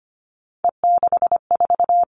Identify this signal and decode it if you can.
Morse code, heard as a keyed tone, about 25 wpm, 700 Hz. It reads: E64
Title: Three Challenging Letter-Number Combinations 25wpm